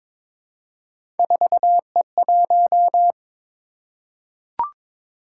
Morse code, heard as a keyed tone, about 22 wpm, 700 Hz. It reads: 4E1
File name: practice / 22wpm / Three Letter-Number Combinations - 22wpm